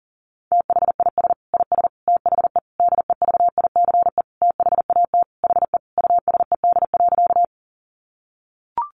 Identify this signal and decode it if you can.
Morse code, heard as a keyed tone, about 40 wpm, 700 Hz. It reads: THIS IS THE DEVICE THAT HE USED.
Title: Sentences from Top 1500 Words 40wpm